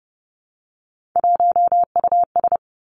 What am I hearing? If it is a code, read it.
Morse code, heard as a keyed tone, about 30 wpm, 700 Hz. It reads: 1US
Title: Three Challenging Letter-Number Combinations 30wpm